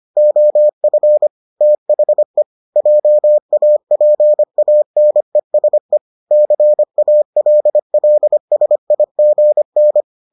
Morse code, heard as a keyed tone, about 25 wpm, 600 Hz. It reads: OF THE JAPANESE CALLSIGN